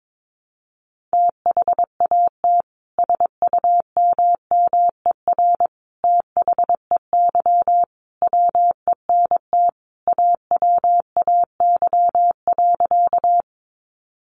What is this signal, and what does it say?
Morse code, heard as a keyed tone, about 22 wpm, 700 Hz. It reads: THAT SUMMER THEY WENT AWAY.
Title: Sentences from Top 1100 Words 22wpm